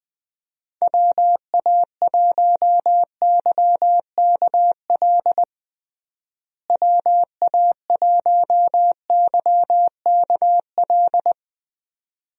Morse code, heard as a keyed tone, about 21 wpm, 700 Hz. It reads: WA1YKL WA1YKL
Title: USA Call Signs - 6 Characters - Repeated 2x - 20wpm